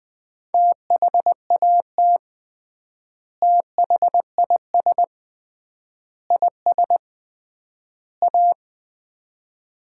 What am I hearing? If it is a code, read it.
Morse code, heard as a keyed tone, about 21 wpm, 700 Hz. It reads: THAT THIS IS A
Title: 3x Word-Spacing - Sets of 4 Words - 20wpm